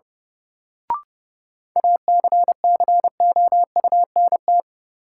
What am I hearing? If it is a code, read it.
Morse code, heard as a keyed tone, about 30 wpm, 700 Hz. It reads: ACCOUNT
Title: Sentences from Top 900 Words - New Word Drill - 30wpm